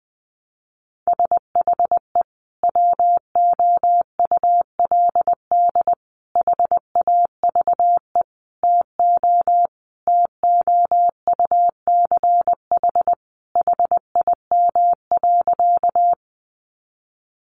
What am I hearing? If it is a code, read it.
Morse code, heard as a keyed tone, about 20 wpm, 700 Hz. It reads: SHE WOULD HAVE TO TOUCH HIM.
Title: Sentences from Top 1200 Words 20wpm